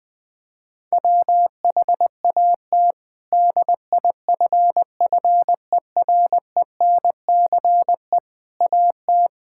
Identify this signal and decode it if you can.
Morse code, heard as a keyed tone, about 21 wpm, 700 Hz. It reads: WHAT DIFFERENCE AT
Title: Sentences with 8-Word Phrases - 20wpm